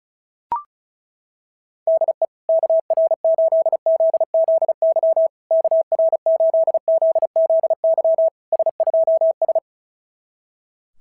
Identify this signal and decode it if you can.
Morse code, heard as a keyed tone, about 33 wpm, 660 Hz. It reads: DE KR8ZZY KR8ZZY S2S
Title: SOTA QSO Practice 35wpm